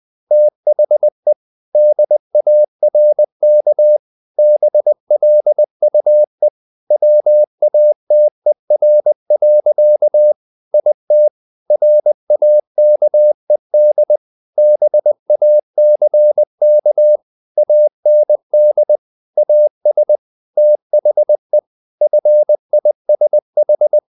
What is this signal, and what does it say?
Morse code, heard as a keyed tone, about 20 wpm, 600 Hz. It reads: THE DARK BLUE WATER. IT RAKED BACK AND AS THE FISH
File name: the old man and the sea part 2 20wpm